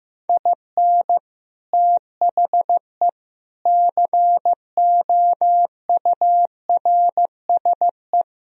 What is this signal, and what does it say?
Morse code, heard as a keyed tone, about 15 wpm, 700 Hz. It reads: IN THE COURSE
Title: Speed-Racing - Sets of 5 Words 15wpm